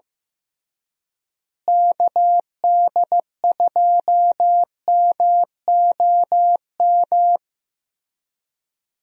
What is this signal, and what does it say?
Morse code, heard as a keyed tone, about 15 wpm, 700 Hz. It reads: KD2MOM
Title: USA Call Signs - 6 Characters - Repeated 3x - 15wpm